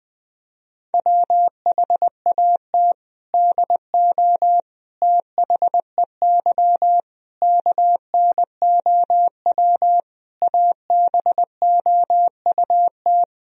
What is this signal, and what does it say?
Morse code, heard as a keyed tone, about 19 wpm, 700 Hz. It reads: WHAT DO THEY KNOW ABOUT
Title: Sentences from Top 1000 Words - Review Part 3 - 20wpm